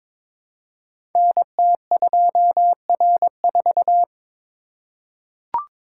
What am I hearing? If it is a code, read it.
Morse code, heard as a keyed tone, about 22 wpm, 700 Hz. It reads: NT2R4
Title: Code Groups - 5 Letter:Numbers - 22wpm